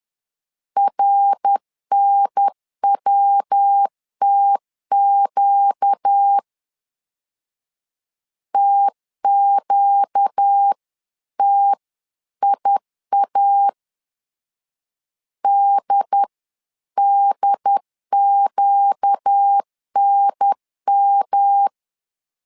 Morse code, heard as a keyed tone, about 10 wpm, 800 Hz. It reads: RNWTQ TQTIA DDQNM
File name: LEZIONI / 4 Q 10wpm 2210